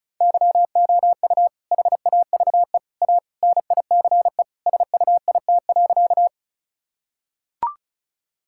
Morse code, heard as a keyed tone, about 36 wpm, 700 Hz. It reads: YOU HAVE A NICE SUIT.
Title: Sentences from Top 1600 Words 35wpm